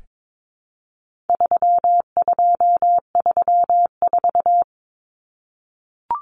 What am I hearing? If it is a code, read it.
Morse code, heard as a keyed tone, about 22 wpm, 700 Hz. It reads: 3234
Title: Common Cut Numbers - 4-Digit 22wpm